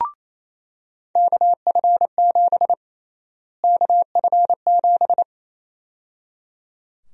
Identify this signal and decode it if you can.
Morse code, heard as a keyed tone, about 28 wpm, 700 Hz. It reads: KF7 KF7
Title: USA Call Sign Prefixes - Repeated 2x - 28wpm